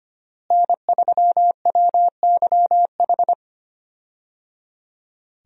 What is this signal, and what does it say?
Morse code, heard as a keyed tone, about 25 wpm, 700 Hz. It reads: N3WYH
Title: USA Call Signs - 5 Characters - Repeated 3x - 25wpm